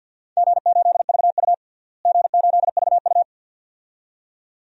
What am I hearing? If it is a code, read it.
Morse code, heard as a keyed tone, about 47 wpm, 700 Hz. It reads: G8VU G8VU
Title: DXCC-100 Call Signs - 4 Characters - Repeated 2x - 50wpm